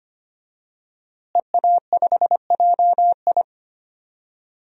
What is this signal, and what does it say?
Morse code, heard as a keyed tone, about 25 wpm, 700 Hz. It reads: EA5JI
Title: DXCC-100 Call Signs - 5 Characters - Repeated 3x - 25wpm